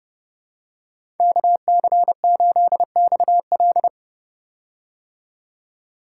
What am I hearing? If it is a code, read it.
Morse code, heard as a keyed tone, about 30 wpm, 700 Hz. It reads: KC8XL
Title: Code Groups - 5 Letter:Numbers - 30wpm